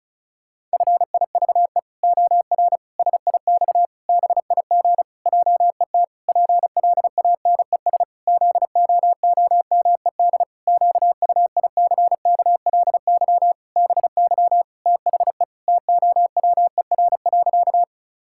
Morse code, heard as a keyed tone, about 36 wpm, 700 Hz. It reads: FIVE OR SIX BIG JET PLANES ZOOMED QUICKLY BY THE TOWER.
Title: Pangram Challenge v2 - 35wpm